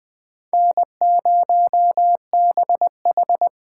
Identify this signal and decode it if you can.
Morse code, heard as a keyed tone, about 21 wpm, 700 Hz. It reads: N0BH
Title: Speed Racing - USA 1x2 Call Signs 17wpm